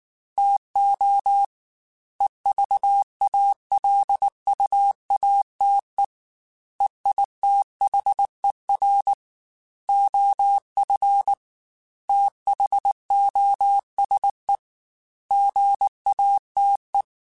Morse code, heard as a keyed tone, about 19 wpm, 800 Hz. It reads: TO EVALUATE EITHER OF THOSE GATE
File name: Technical Subjects / Technical Subjects-mp3s / AND Gate @18